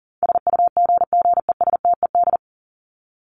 Morse code, heard as a keyed tone, about 40 wpm, 700 Hz. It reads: SUGGESTED
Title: ED Suffix - Top 100 Words with Base Words 40wpm